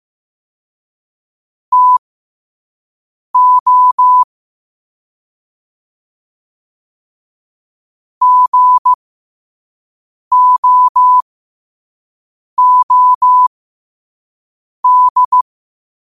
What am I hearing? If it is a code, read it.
Morse code, heard as a keyed tone, about 15 wpm, spaced as for 5 wpm, 1000 Hz. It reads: TO GOOD
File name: qotdfiles / 5wpm